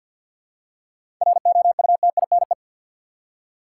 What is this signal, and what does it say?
Morse code, heard as a keyed tone, about 51 wpm, 700 Hz. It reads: ROUTINE
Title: Seven Letter Words - 50wpm